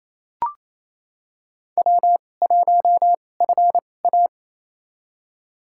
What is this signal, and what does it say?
Morse code, heard as a keyed tone, about 28 wpm, spaced as for 20 wpm, 700 Hz. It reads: W1FA
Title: USA Call Signs - 2x Character Spacing - 28wpm